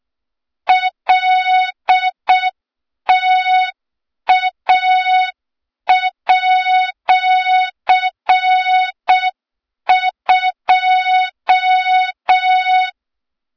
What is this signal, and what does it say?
Morse code, heard as a keyed tone, about 6 wpm, 3700 Hz, with noise. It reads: LTA@2
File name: Morze-HA2MN / hang / 23 GYAKORLAT